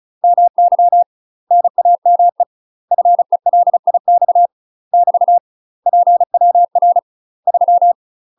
Morse code, heard as a keyed tone, about 36 wpm, 700 Hz. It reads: MY NAME FELIX <BT> PWR 3
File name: UK MASTER QSOs TRAINING / 35WPM / UK MASTER QSOs 0022